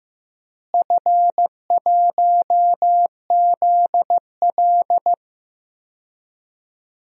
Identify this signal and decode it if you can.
Morse code, heard as a keyed tone, about 15 wpm, 700 Hz. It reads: F1ZL
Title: DXCC-100 Call Signs - 4 Characters - Repeated 2x - 15wpm